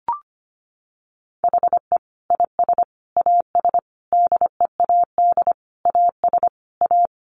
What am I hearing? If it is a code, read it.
Morse code, heard as a keyed tone, about 25 wpm, 700 Hz. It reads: HE IS AS DEAD AS A
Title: Sentences from Top 1500 Words 25wpm